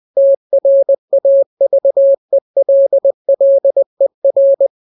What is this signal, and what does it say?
Morse code, heard as a keyed tone, about 20 wpm, 550 Hz. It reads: TRAVELLER